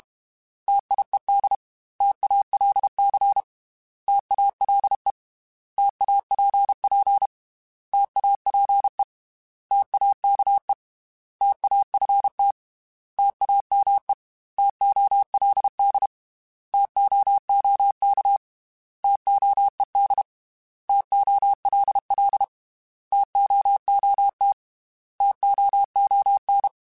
Morse code, heard as a keyed tone, about 31 wpm, 800 Hz. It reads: TIED TALC TALE TAPP TAPE TAKE TAFT TAME TOLD TOOK TOED TOLL TOOT TOON
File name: T Prefixes / T mp3s